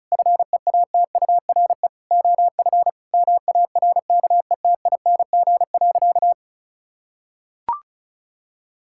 Morse code, heard as a keyed tone, about 36 wpm, 700 Hz. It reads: FEATURE OF MARKETING.
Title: Sentences from Top 1000 Words - Review Part 6 - 35wpm